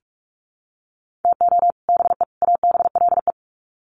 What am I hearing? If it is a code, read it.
Morse code, heard as a keyed tone, about 44 wpm, 700 Hz. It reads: TO BE ABLE
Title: Sets of 3 Words 45wpm